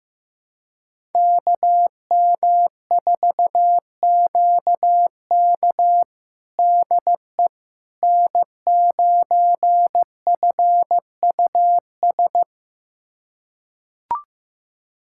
Morse code, heard as a keyed tone, about 15 wpm, 700 Hz. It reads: KM4QK DE N9FUS
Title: USA Callsign DE Callsign - 5 Characters - 15wpm